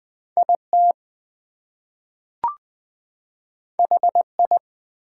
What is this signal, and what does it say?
Morse code, heard as a keyed tone, about 20 wpm, 700 Hz. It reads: IT HI
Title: Two Letter Words - 20wpm